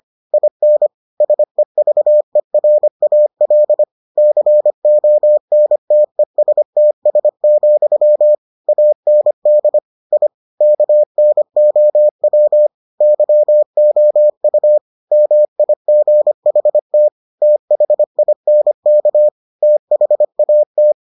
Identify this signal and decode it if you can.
Morse code, heard as a keyed tone, about 25 wpm, 600 Hz. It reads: IN SEVERAL CONTESTS, AND I KNOW YOU MIGHT THINK THAT